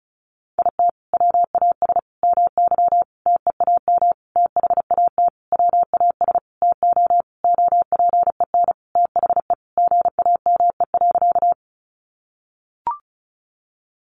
Morse code, heard as a keyed tone, about 34 wpm, 700 Hz. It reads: IT WAS MY TEAM THAT WAS TO OPEN THE GAME.
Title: Sentences from Top 1500 Words 35wpm